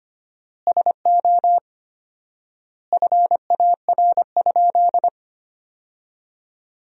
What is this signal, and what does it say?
Morse code, heard as a keyed tone, about 25 wpm, 700 Hz. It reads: SO FAR?
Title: 4x Word-Spacing - Sentences from Top 300 Words - 25wpm